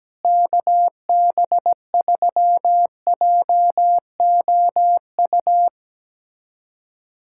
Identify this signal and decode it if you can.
Morse code, heard as a keyed tone, about 17 wpm, 700 Hz. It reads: KB3JOU
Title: USA Callsign DE Callsign - 6 Characters - 17wpm